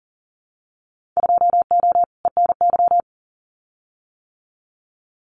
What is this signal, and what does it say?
Morse code, heard as a keyed tone, about 40 wpm, 700 Hz. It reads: 2O ENY
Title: Winter Field Day Exchanges Encore 40wpm